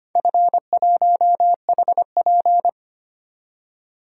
Morse code, heard as a keyed tone, about 25 wpm, 700 Hz. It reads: F1HP
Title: DXCC-100 Call Signs - 4 Characters - 25wpm